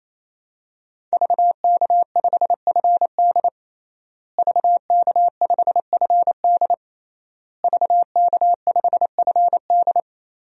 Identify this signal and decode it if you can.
Morse code, heard as a keyed tone, about 28 wpm, 700 Hz. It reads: VK5FD VK5FD VK5FD
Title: DXCC-100 Call Signs - 5 Characters - Repeated 3x - 28wpm